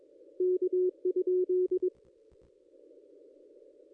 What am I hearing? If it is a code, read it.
Morse code, heard as a keyed tone, about 22 wpm, 370 Hz. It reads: K?
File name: contests / ss-cw-k5tr-op-n5rz -1800